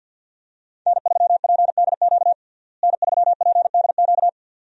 Morse code, heard as a keyed tone, about 51 wpm, 700 Hz. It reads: N3PDQ N3PDQ
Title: USA Call Signs - 5 Characters - Repeated 2x - 50wpm